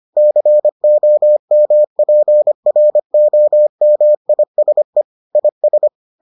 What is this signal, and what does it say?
Morse code, heard as a keyed tone, about 25 wpm, 600 Hz. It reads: COMPROMISE IS